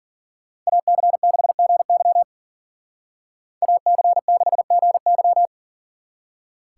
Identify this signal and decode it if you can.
Morse code, heard as a keyed tone, about 49 wpm, 700 Hz. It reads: AC6GY AC6GY
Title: Speed Racing - USA 2x2 Call Signs 40wpm